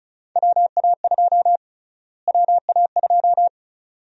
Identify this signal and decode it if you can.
Morse code, heard as a keyed tone, about 36 wpm, 700 Hz. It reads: WA2 WA2
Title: USA Call Sign Prefixes - Repeated 2x - 35wpm